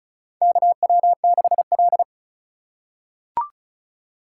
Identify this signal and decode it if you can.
Morse code, heard as a keyed tone, about 34 wpm, 700 Hz. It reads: KW6L
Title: USA Call Signs - 4 Characters - 4x Character Spacing - 35wpm